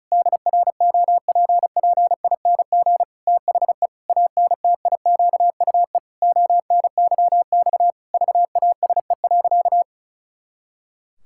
Morse code, heard as a keyed tone, about 36 wpm, 700 Hz. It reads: DROPPING THE ANTIQUE ONYX VASE.
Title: Pangram Challenge v2 - 35wpm